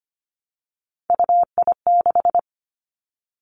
Morse code, heard as a keyed tone, about 25 wpm, 700 Hz. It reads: UI6
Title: Three Challenging Letter-Number Combinations 25wpm